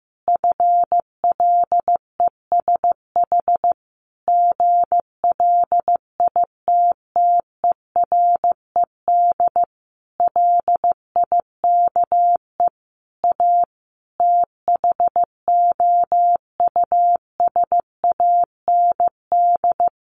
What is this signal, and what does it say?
Morse code, heard as a keyed tone, about 14 wpm, 700 Hz. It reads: FLESH GLITTERED LIKE A THOUSAND
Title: A Sound Of Thunder 15wpm